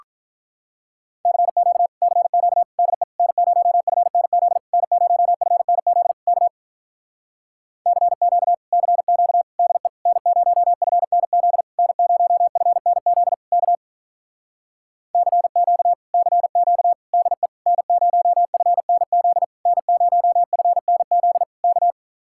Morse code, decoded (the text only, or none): CQ CQ DE N0FNZ N0FNZ K CQ CQ DE N0FNZ N0FNZ K CQ CQ DE N0FNZ N0FNZ K